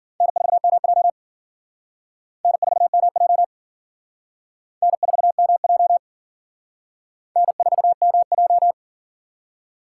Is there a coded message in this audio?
Yes: Morse code from 0.2 to 8.7 s, heard as a keyed tone, about 50 wpm, 700 Hz. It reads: N4MJ N4MJ N4MJ N4MJ